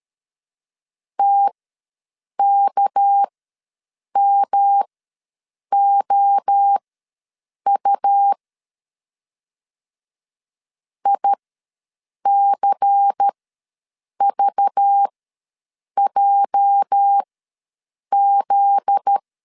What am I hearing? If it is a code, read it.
Morse code, heard as a keyed tone, about 13 wpm, spaced as for 7 wpm, 800 Hz. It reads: TKMOU ICVJZ
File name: LEZIONI / 8 YJ45 / 8 YJ ws 12wpm 3342